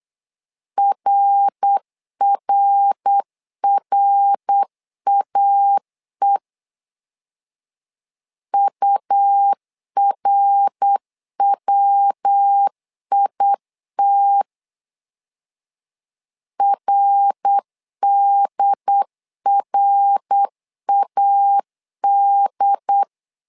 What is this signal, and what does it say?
Morse code, heard as a keyed tone, about 8 wpm, 800 Hz. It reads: RRRAE URWIT RDRAD
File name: LEZIONI / 3 MWRO / 3 MWR 8wpm 2444